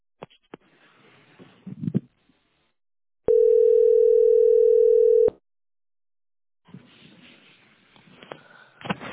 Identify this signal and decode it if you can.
Ringback tone (North America), 440 + 480 Hz.